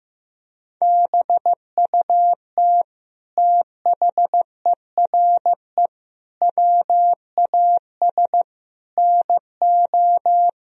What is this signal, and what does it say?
Morse code, heard as a keyed tone, about 15 wpm, 700 Hz. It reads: BUT THERE WAS NO
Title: Sentences from Top 1000 Words - Review Part 3 - 15wpm